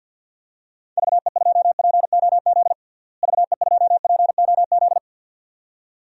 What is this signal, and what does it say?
Morse code, heard as a keyed tone, about 47 wpm, 700 Hz. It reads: VE2POZ VE2POZ
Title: DXCC-100 Call Signs - 6 Characters - Repeated 2x - 50wpm